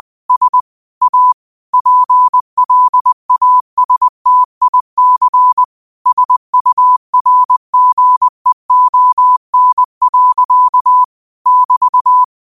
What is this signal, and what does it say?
Morse code, heard as a keyed tone, about 20 wpm, 1000 Hz. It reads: S A PLASTIC SURGEON. <BT>
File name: qotdfiles / 20wpm